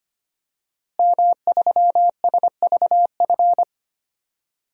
Morse code, heard as a keyed tone, about 25 wpm, 700 Hz. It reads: M3SVF
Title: DXCC-100 Call Signs - 5 Characters - Repeated 2x - 25wpm